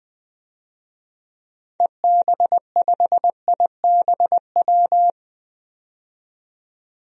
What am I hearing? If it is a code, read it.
Morse code, heard as a keyed tone, about 20 wpm, 700 Hz. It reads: EB5IBW
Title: DXCC-100 Call Signs - 6 Characters - 20wpm